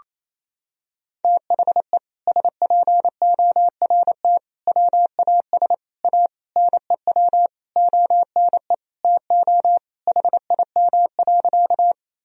Morse code, heard as a keyed tone, about 28 wpm, 700 Hz. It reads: THE SPORT WAS A NEW ONE TO HIM.
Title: Sentences from Top 1000 Words - Review Part 4 - 28wpm